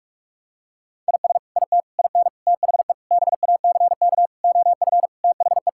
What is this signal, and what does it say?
Morse code, heard as a keyed tone, about 46 wpm, 700 Hz. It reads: IS IT IN THE BACK OF THE
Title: Sentences from Top 1000 Words - Review Part 5 - 45wpm